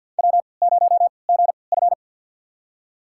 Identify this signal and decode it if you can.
Morse code, heard as a keyed tone, about 53 wpm, spaced as for 29 wpm, 700 Hz. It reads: W0GF
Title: USA Call Signs - 4 Characters - 3x Character Spacing - 50wpm